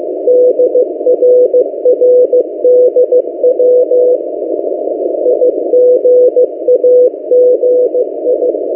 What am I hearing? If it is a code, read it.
Morse code, heard as a keyed tone, about 16 wpm, 500 Hz. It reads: DRRDW IGAGI
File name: M08a- -1355utc-8096khz